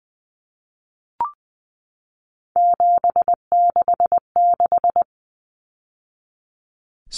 Morse code, heard as a keyed tone, about 20 wpm, 700 Hz. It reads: 766
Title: Numbers - 3-Digit 20wpm